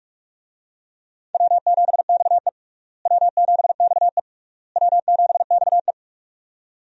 Morse code, heard as a keyed tone, about 45 wpm, 700 Hz. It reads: W7XE W7XE W7XE
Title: USA Call Signs - 4 Characters - Repeated 3x - 45wpm